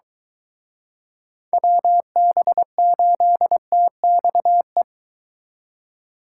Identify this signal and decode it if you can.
Morse code, heard as a keyed tone, about 23 wpm, 700 Hz. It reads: WB8TXE